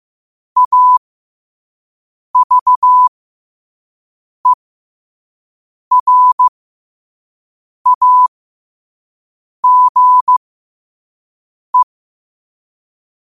Morse code, heard as a keyed tone, about 15 wpm, spaced as for 5 wpm, 1000 Hz. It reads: AVERAGE
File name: qotdfiles / 5wpm